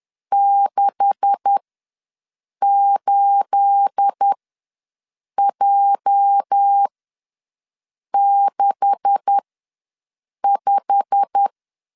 Morse code, heard as a keyed tone, about 10 wpm, spaced as for 6 wpm, 800 Hz. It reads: 68J65
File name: LEZIONI / 9 286 ws 10wpm 4032